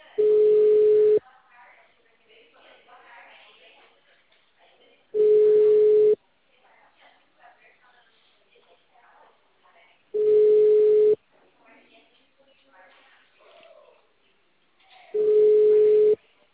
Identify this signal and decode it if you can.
Ringback tone (Europe, 425 Hz), with speech.
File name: out- -2007- - - .45801